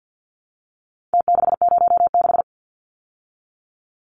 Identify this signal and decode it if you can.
Morse code, heard as a keyed tone, about 50 wpm, 700 Hz. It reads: T606
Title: Numbers - Contest Style - 4-Digit 50wpm